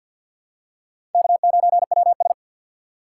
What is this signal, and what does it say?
Morse code, heard as a keyed tone, about 49 wpm, 700 Hz. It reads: K9WS
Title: USA Call Signs - 6x Character Spacing - 50wpm